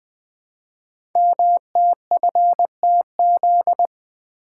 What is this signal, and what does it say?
Morse code, heard as a keyed tone, about 20 wpm, 700 Hz. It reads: MTFTZ
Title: Code Groups - 5 Letter:Numbers - 20wpm